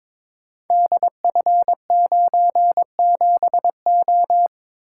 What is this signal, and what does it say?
Morse code, heard as a keyed tone, about 23 wpm, 700 Hz. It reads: DF97O
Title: Code Groups - 5 Letter:Numbers - 22wpm